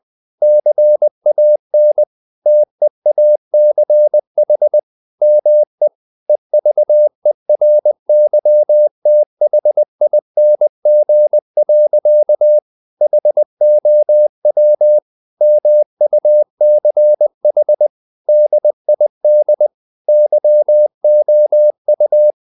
Morse code, heard as a keyed tone, about 20 wpm, 600 Hz. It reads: CAN TEACH ME EVERYTHING. HOW MUCH DID YOU
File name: the old man and the sea part 2 20wpm